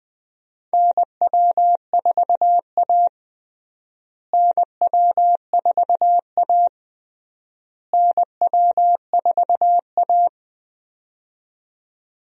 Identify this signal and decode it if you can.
Morse code, heard as a keyed tone, about 20 wpm, 700 Hz. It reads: NW4A NW4A NW4A
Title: USA Call Signs - 4 Characters - Repeated 3x - 20wpm